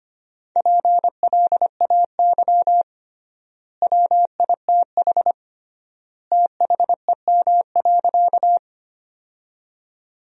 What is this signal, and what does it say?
Morse code, heard as a keyed tone, about 25 wpm, 700 Hz. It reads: PLAY WITH THEM.
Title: 3x Word-Spacing - Sentences from Top 200 Words - 25wpm